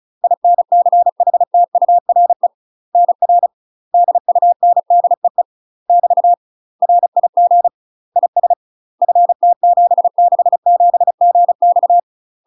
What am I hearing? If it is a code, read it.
Morse code, heard as a keyed tone, about 36 wpm, 700 Hz. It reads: INCHTURE NR DUNDEE <BT> RIG IS FT767GX
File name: UK MASTER QSOs TRAINING / 35WPM / UK MASTER QSOs 0026